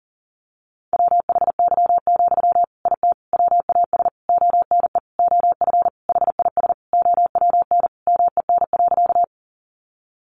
Morse code, heard as a keyed tone, about 40 wpm, 700 Hz. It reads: WHY, IT WAS ONE OF HIS OWN MEN.
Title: Sentences from Top 300 Words 40wpm